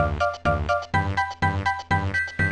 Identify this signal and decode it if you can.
There is no code.